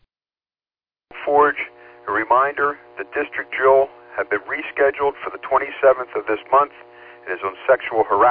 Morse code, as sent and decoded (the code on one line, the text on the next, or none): none